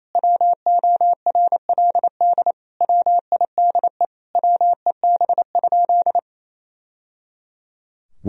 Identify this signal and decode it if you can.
Morse code, heard as a keyed tone, about 27 wpm, 700 Hz. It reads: WORLD WIDE WEB?
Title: Sentences from Top 1000 Words - Review Part 4 - 28wpm